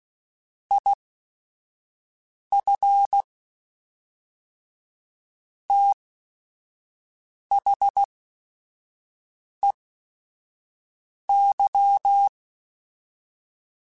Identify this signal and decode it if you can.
Morse code, heard as a keyed tone, about 16 wpm, spaced as for 5 wpm, 800 Hz. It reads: IF THEY
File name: Quotes / Quotes-mp3s / Advice @5